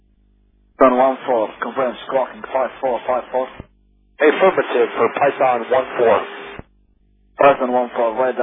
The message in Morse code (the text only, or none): none